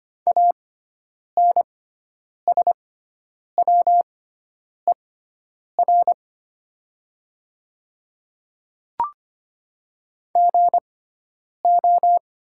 Morse code, heard as a keyed tone, about 25 wpm, spaced as for 9 wpm, 700 Hz. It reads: ANSWER GO
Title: 6x Character Spacing - Top 300 Words - 25wpm